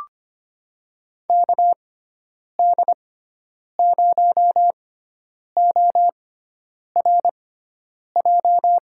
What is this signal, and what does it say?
Morse code, heard as a keyed tone, about 25 wpm, spaced as for 9 wpm, 700 Hz. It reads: KD0ORJ
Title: USA Call Signs - 6x Character Spacing - 25wpm